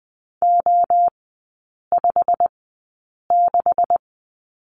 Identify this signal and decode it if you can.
Morse code, heard as a keyed tone, about 20 wpm, spaced as for 8 wpm, 690 Hz. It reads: O56
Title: 212 Lesson 34e - Number 8 - Three Characters 20wpm